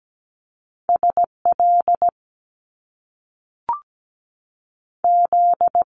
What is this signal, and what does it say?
Morse code, heard as a keyed tone, about 17 wpm, 700 Hz. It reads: S L Z
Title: Top Two Letter Combinations in Words 17wpm